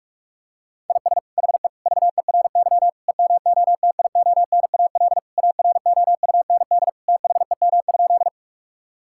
Morse code, heard as a keyed tone, about 46 wpm, 700 Hz. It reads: IS HE VERY EMOTIONAL AROUND THEM?
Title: Sentences from Top 1000 Words - Review Part 5 - 45wpm